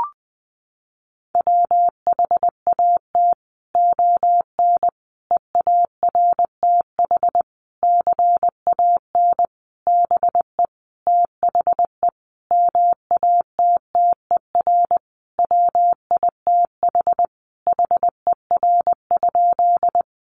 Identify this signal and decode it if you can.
Morse code, heard as a keyed tone, about 20 wpm, 700 Hz. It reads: WHAT ON EARTH CAN BE THE MATTER WITH HER?
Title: Sentences from Top 700 Words 20wpm